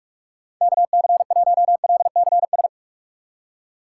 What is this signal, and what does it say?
Morse code, heard as a keyed tone, about 43 wpm, 700 Hz. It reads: KC1LCS
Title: USA Call Signs - 6 Characters - 6x Character Spacing - 45wpm